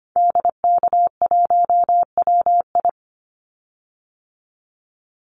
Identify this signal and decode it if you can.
Morse code, heard as a keyed tone, about 25 wpm, 700 Hz. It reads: DK1WI
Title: Call Signs Repeated 2x - 25wpm